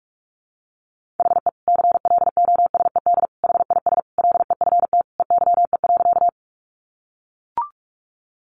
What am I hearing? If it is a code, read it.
Morse code, heard as a keyed tone, about 44 wpm, 700 Hz. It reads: HE CLOSED HIS LEFT EYE.
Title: Sentences from Top 1400 Words 45wpm